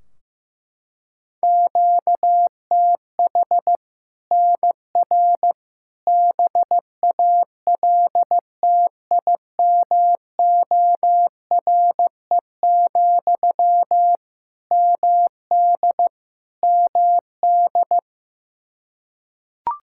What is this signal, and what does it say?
Morse code, heard as a keyed tone, about 15 wpm, 700 Hz. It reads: QTH NR BALTIMORE, MD MD
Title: Practice QSO - Exchange 1 - Beginning - 15wpm